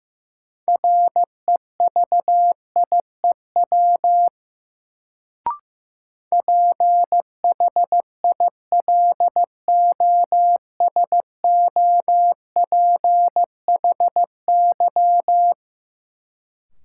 Morse code, heard as a keyed tone, about 15 wpm, 700 Hz. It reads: REVIEW PHILOSOPHY
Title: Sentences from Top 900 Words - New Word Drill - 15wpm